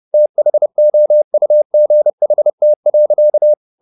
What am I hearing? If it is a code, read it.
Morse code, heard as a keyed tone, about 30 wpm, 600 Hz. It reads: THOUGHT.